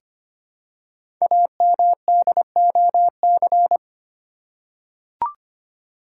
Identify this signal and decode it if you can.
Morse code, heard as a keyed tone, about 25 wpm, 700 Hz. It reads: AMDOC
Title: Code Groups - 5 Letters - 25wpm